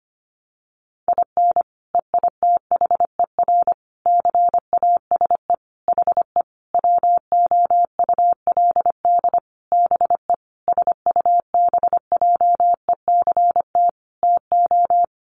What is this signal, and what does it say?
Morse code, heard as a keyed tone, about 25 wpm, 700 Hz. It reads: IN EITHER CASE HE WOULD BE SUBJECT TO
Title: Sentences from Top 1300 Words 25wpm